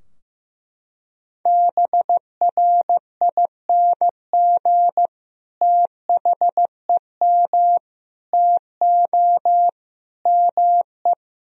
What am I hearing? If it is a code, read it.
Morse code, heard as a keyed tone, about 15 wpm, 700 Hz. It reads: BRING THEM TO ME
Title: Sentences from Top 1000 Words - Review Part 3 - 15wpm